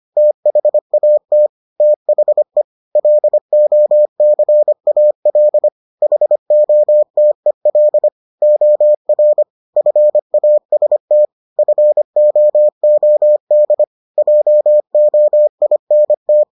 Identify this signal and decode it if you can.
Morse code, heard as a keyed tone, about 25 wpm, 600 Hz. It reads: THAT THE LOCAL HOTEL OR FAST FOOD JOINT